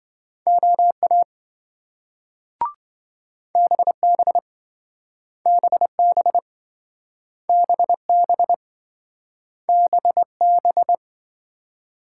Morse code, heard as a keyed tone, about 27 wpm, 700 Hz. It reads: OA BB BB BB BB